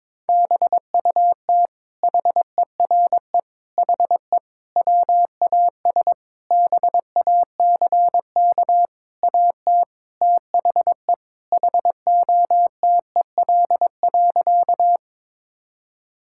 Morse code, heard as a keyed tone, about 22 wpm, 700 Hz. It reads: BUT HERE HE WAS BACK AT THE HOTEL.
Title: Sentences from Top 1600 Words 22wpm